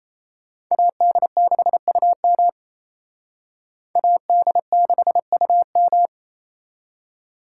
Morse code, heard as a keyed tone, about 35 wpm, 700 Hz. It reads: AD6UM AD6UM
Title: Speed Racing - USA 2x2 Call Signs 28wpm